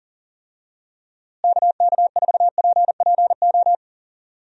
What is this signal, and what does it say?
Morse code, heard as a keyed tone, about 40 wpm, 700 Hz. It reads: KK4PPO
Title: USA Call Signs - 6 Characters - Repeated 2x - 40wpm